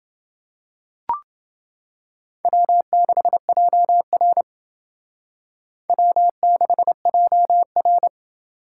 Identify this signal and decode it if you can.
Morse code, heard as a keyed tone, about 28 wpm, 700 Hz. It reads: W6JR W6JR